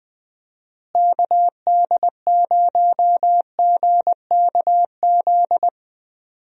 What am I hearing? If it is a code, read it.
Morse code, heard as a keyed tone, about 21 wpm, 700 Hz. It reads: KD0GKZ
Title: USA Call Signs - 6 Characters - Repeated 3x - 20wpm